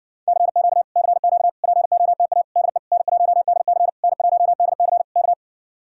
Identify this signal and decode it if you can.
Morse code, heard as a keyed tone, about 69 wpm, 700 Hz. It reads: CQ CQ POTA DE N1DC N1DC K